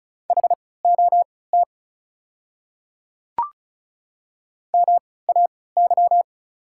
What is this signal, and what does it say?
Morse code, heard as a keyed tone, about 36 wpm, spaced as for 20 wpm, 690 Hz. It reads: HOT MAY
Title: 3x Character Spacing - Top 100 Words - 35wpm